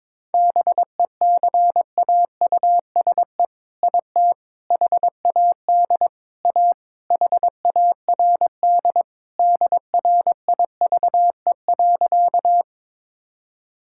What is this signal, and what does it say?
Morse code, heard as a keyed tone, about 22 wpm, 700 Hz. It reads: BECAUSE IT HAD A HARD DRIVE.
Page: morse-code-ninja-practice-22wpm